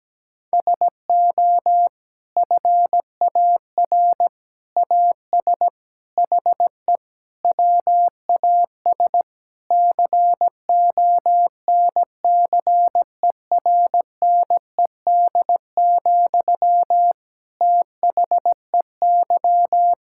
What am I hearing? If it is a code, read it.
Morse code, heard as a keyed tone, about 17 wpm, 700 Hz. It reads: SO FAR AS HE WAS CONCERNED, THEY
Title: Sentences from Top 1000 Words - Review Part 6 - 17wpm